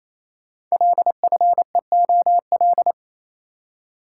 Morse code, heard as a keyed tone, about 28 wpm, 700 Hz. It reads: LFEOL
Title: Code Groups - 5 Letters - 28wpm